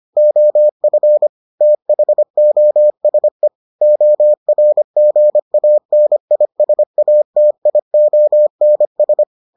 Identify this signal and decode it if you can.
Morse code, heard as a keyed tone, about 25 wpm, 600 Hz. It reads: OF THOSE ORGANISATIONS